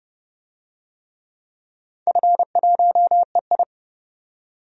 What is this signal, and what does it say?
Morse code, heard as a keyed tone, about 30 wpm, 700 Hz. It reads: F1EI
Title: DXCC-100 Call Signs - 4 Characters - 30wpm